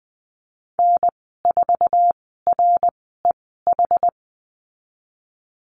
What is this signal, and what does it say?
Morse code, heard as a keyed tone, about 20 wpm, spaced as for 15 wpm, 700 Hz. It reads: N4REH
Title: 122 Lesson 20g - Number 3 - Call Signs 20wpm